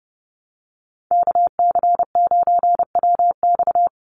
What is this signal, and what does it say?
Morse code, heard as a keyed tone, about 30 wpm, 700 Hz. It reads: KC9WX
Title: USA Call Signs 30wpm